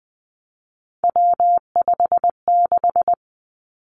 Morse code, heard as a keyed tone, about 19 wpm, 700 Hz. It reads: W56
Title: Three Challenging Letter-Number Combinations 20wpm